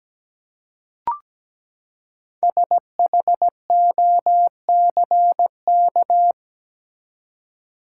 Morse code, heard as a keyed tone, about 17 wpm, 700 Hz. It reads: SHOCK
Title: Five Letter Words - 17wpm